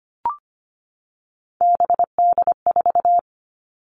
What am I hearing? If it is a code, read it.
Morse code, heard as a keyed tone, about 25 wpm, 700 Hz. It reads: BD4
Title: Callsign Prefixes for Top 100 DXCC - 25wpm